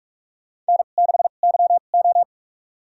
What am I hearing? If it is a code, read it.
Morse code, heard as a keyed tone, about 46 wpm, spaced as for 33 wpm, 700 Hz. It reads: N6YO
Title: USA Call Signs - 4 Characters - 2x Character Spacing - 45wpm